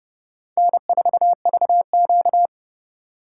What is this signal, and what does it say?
Morse code, heard as a keyed tone, about 37 wpm, 700 Hz. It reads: N4VQ
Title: Speed Racing - USA 1x2 Call Signs 30wpm